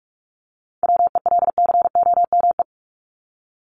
Morse code, heard as a keyed tone, about 44 wpm, 700 Hz. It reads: WELCOME
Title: Top 1751-2000 Words 45wpm